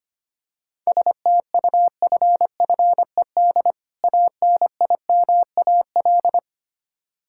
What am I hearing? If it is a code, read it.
Morse code, heard as a keyed tone, about 25 wpm, 700 Hz. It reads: STUFFED ANIMAL
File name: practice / 25wpm / Difficult Words with Hints - 25wpm